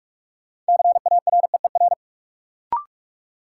Morse code, heard as a keyed tone, about 46 wpm, 700 Hz. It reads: CAREER
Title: Sentences from Top 1000 Words - New Word Drill - 45wpm